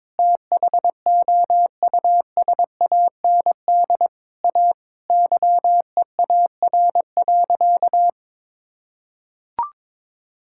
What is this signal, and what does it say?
Morse code, heard as a keyed tone, about 21 wpm, 700 Hz. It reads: THOUSAND A YEAR.
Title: Sentences from Top 1000 Words - Review Part 5 - 22wpm